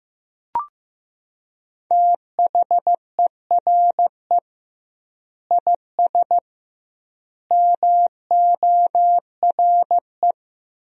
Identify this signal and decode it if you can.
Morse code, heard as a keyed tone, about 15 wpm, 700 Hz. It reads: THERE IS MORE
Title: 2x Word-Spacing - Sentences from Top 200 Words - 15wpm